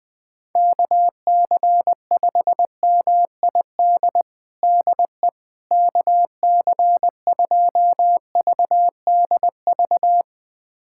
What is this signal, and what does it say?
Morse code, heard as a keyed tone, about 20 wpm, 700 Hz. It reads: KC5MID DE KC2VDV
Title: USA Callsign DE Callsign - 6 Characters - 20wpm